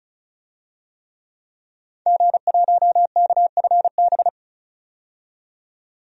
Morse code, heard as a keyed tone, about 36 wpm, 700 Hz. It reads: G1KFB
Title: DXCC-100 Call Signs - 5 Characters - 35wpm